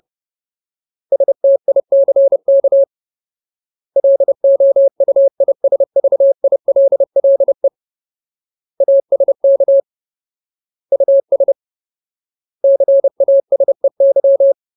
Morse code, heard as a keyed tone, about 30 wpm, 550 Hz. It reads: STICK LOUISVILLE ASK US CASEY
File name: Ultimate Word List / 30WPM Ultimate